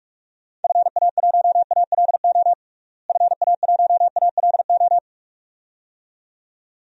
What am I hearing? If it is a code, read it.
Morse code, heard as a keyed tone, about 41 wpm, 700 Hz. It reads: FA1ALO FA1ALO
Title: DXCC-100 Call Signs - 6 Characters - Repeated 2x - 45wpm